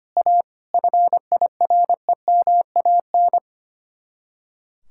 Morse code, heard as a keyed tone, about 29 wpm, 700 Hz. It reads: A FIREMAN
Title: Speed-Racing Common QSO Phrases Encore - 25wpm